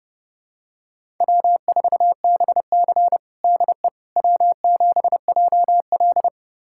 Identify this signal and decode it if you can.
Morse code, heard as a keyed tone, about 32 wpm, 700 Hz. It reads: W4BC DE W7JL
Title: USA Callsign DE Callsign - 4 Characters - 30wpm